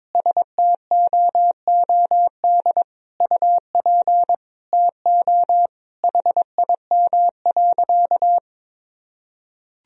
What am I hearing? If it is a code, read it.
Morse code, heard as a keyed tone, about 22 wpm, 700 Hz. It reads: STOOD UP TO HIM.
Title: Sentences from Top 1000 Words - Review Part 2 - 22wpm